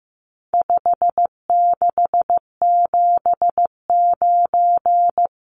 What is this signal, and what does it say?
Morse code, heard as a keyed tone, about 15 wpm, 700 Hz. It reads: 5679
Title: Numbers - 4-Digit 15wpm